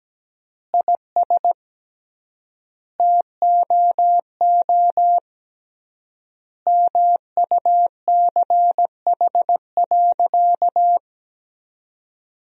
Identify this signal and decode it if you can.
Morse code, heard as a keyed tone, about 17 wpm, 700 Hz. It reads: IS TOO MUCH.
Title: 3x Word-Spacing - Sentences from Top 400 Words - 17wpm